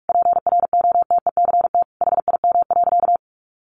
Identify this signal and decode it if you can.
Morse code, heard as a keyed tone, about 44 wpm, 700 Hz. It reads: PROTECT HIM.
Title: Sentences from Top 900 Words 45wpm